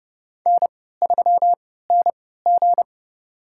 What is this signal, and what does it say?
Morse code, heard as a keyed tone, about 32 wpm, spaced as for 17 wpm, 700 Hz. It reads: N3NG
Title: USA Call Signs - 4 Characters - 3x Character Spacing - 30wpm